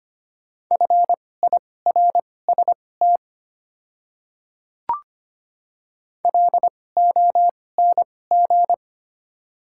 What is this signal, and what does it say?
Morse code, heard as a keyed tone, about 25 wpm, spaced as for 18 wpm, 700 Hz. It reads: FIRST LONG
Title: 2x Character Spacing - Top 100 Words - 25wpm